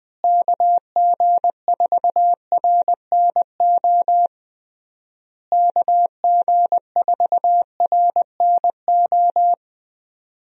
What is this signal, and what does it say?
Morse code, heard as a keyed tone, about 21 wpm, 700 Hz. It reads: KG4RNO KG4RNO
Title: USA Call Signs - 6 Characters - Repeated 3x - 20wpm